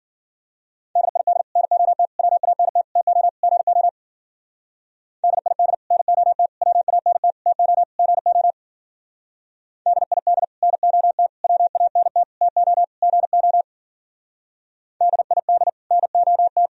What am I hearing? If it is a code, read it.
Morse code, heard as a keyed tone, about 51 wpm, 700 Hz. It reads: DID NOT WANT TO GO DID NOT WANT TO GO DID NOT WANT TO GO DID NOT